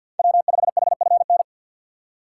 Morse code, heard as a keyed tone, about 49 wpm, 700 Hz. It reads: W5HFN
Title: USA Call Signs - 5 Characters - 4x Character Spacing - 50wpm